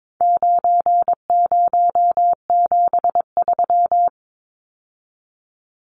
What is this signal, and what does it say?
Morse code, heard as a keyed tone, about 22 wpm, 700 Hz. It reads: 9073
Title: Numbers - 4-Digit 22wpm